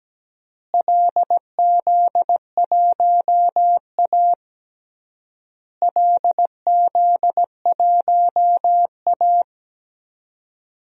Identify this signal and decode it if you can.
Morse code, heard as a keyed tone, about 17 wpm, 700 Hz. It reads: LZ1A LZ1A
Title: DXCC-100 Call Signs - 4 Characters - Repeated 3x - 17wpm